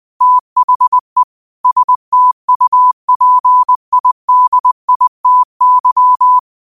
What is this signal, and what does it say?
Morse code, heard as a keyed tone, about 20 wpm, 1000 Hz. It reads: THE STUPIDITY
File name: qotdfiles / 20wpm